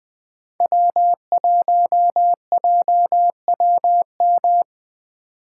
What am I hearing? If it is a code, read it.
Morse code, heard as a keyed tone, about 20 wpm, 700 Hz. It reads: W1JWM
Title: USA Call Signs - 2x Character Spacing - 20wpm